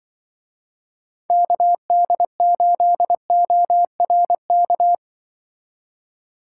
Morse code, heard as a keyed tone, about 23 wpm, 700 Hz. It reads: KD8ORK
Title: Speed Racing - USA Call Signs 20wpm